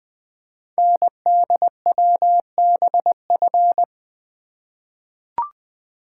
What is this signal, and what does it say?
Morse code, heard as a keyed tone, about 20 wpm, 700 Hz. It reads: NDWBF
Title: Code Groups - 5 Letters - 20wpm